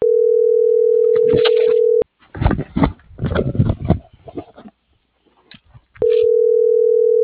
Ringback tone (North America), 440 + 480 Hz.